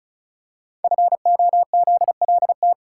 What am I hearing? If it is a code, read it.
Morse code, heard as a keyed tone, about 36 wpm, 700 Hz. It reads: FOZLT
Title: Code Groups - 5 Letters - 35wpm